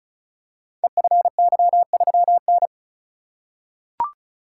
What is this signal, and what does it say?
Morse code, heard as a keyed tone, about 36 wpm, 700 Hz. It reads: EFY3N
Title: Code Groups - 5 Letter:Numbers - 35wpm